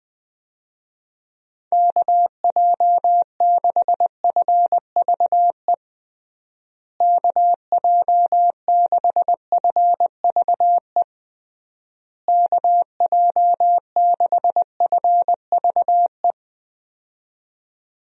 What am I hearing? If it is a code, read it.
Morse code, heard as a keyed tone, about 20 wpm, 700 Hz. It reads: KJ6FVE KJ6FVE KJ6FVE
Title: USA Call Signs - 6 Characters - Repeated 3x - 20wpm